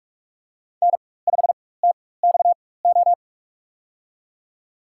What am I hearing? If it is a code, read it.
Morse code, heard as a keyed tone, about 46 wpm, spaced as for 21 wpm, 700 Hz. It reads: N5TXO
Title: USA Call Signs - 5 Characters - 4x Character Spacing - 45wpm